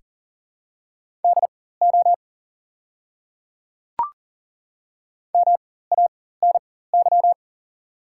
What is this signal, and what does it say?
Morse code, heard as a keyed tone, about 40 wpm, spaced as for 19 wpm, 700 Hz. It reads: DO MANY
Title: 4x Character Spacing - Top 200 Words - 40wpm